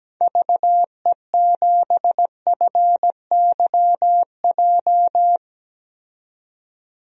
Morse code, heard as a keyed tone, about 17 wpm, 700 Hz. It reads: VE7FYJ
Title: DXCC-100 Call Signs - 6 Characters - 17wpm